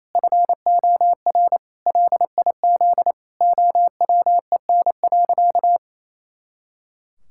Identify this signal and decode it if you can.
Morse code, heard as a keyed tone, about 28 wpm, 700 Hz. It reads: FOR LIZ OWEN.
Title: Pangram Challenge v2 - 28wpm